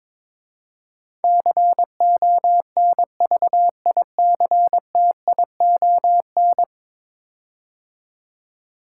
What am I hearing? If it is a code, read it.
Morse code, heard as a keyed tone, about 22 wpm, 700 Hz. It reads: CONVICTION
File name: practice / 22wpm / 10 Letter Words - First 3 Letters Given - 22wpm